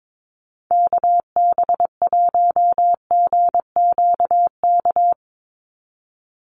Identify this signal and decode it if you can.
Morse code, heard as a keyed tone, about 22 wpm, 700 Hz. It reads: KB1GQK
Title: USA Call Signs 22wpm